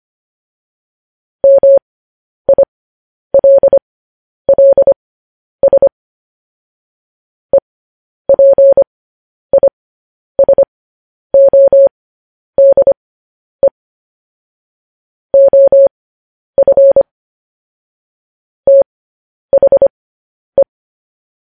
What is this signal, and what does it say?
Morse code, heard as a keyed tone, about 25 wpm, spaced as for 10 wpm, 550 Hz. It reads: MILLS EPISODE OF THE